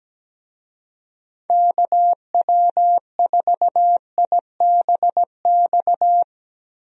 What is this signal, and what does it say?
Morse code, heard as a keyed tone, about 17 wpm, 700 Hz. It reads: KW4IBX
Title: USA Call Signs - 6 Characters - Repeated 3x - 17wpm